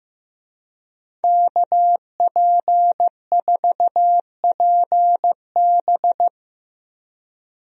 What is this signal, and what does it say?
Morse code, heard as a keyed tone, about 16 wpm, 700 Hz. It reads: KP4PB
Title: DXCC-100 Call Signs - 5 Characters - 15wpm